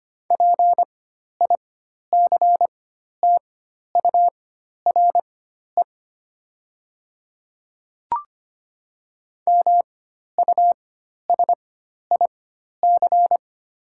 Morse code, heard as a keyed tone, about 25 wpm, spaced as for 12 wpm, 700 Hz. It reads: PICTURE MUSIC
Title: 4x Character Spacing - Top 300 Words - 25wpm